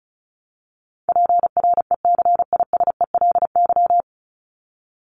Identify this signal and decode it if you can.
Morse code, heard as a keyed tone, about 34 wpm, 700 Hz. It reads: PRECISELY
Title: LY Suffix - Top 100 Words with Base Words 35wpm